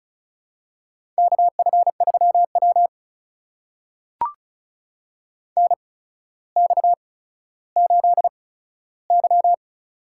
Morse code, heard as a keyed tone, about 36 wpm, spaced as for 15 wpm, 700 Hz. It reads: KF3W NX8Y
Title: USA Call Signs - 4 Characters - 8x Character Spacing - 35wpm